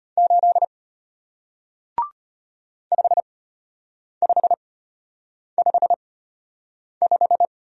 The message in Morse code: 8 5555

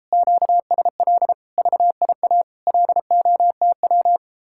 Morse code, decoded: QSL VIA LOTW

